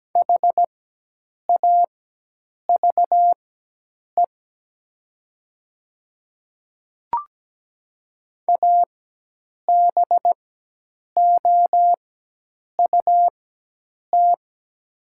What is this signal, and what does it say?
Morse code, heard as a keyed tone, about 17 wpm, spaced as for 8 wpm, 700 Hz. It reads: HAVE ABOUT
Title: 4x Character Spacing - Top 100 Words - 17wpm